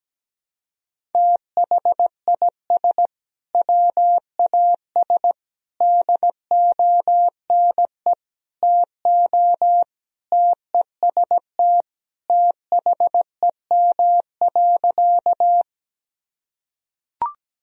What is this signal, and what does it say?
Morse code, heard as a keyed tone, about 17 wpm, 700 Hz. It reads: THIS WAS DONE TO TEST THEM.
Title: Sentences from Top 1000 Words - Review Part 3 - 17wpm